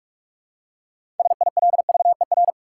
Morse code, heard as a keyed tone, about 42 wpm, 700 Hz. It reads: SILVER
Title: Six Letter Words - 45wpm